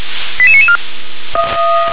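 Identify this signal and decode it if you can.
Five-tone selective call (ZVEI), heard as quick runs of five tones, tones of about 70 ms. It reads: 88994
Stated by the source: ZVEI five-tone call received on an SDR.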